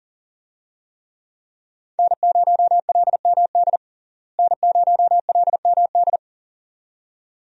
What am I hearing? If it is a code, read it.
Morse code, heard as a keyed tone, about 40 wpm, 700 Hz. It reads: N0LMD N0LMD
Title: USA Call Signs - 5 Characters - Repeated 2x - 40wpm